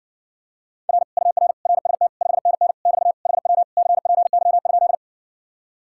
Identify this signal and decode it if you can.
Morse code, heard as a keyed tone, about 60 wpm, 700 Hz. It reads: R UR RST 5NN <BT> HW CPY?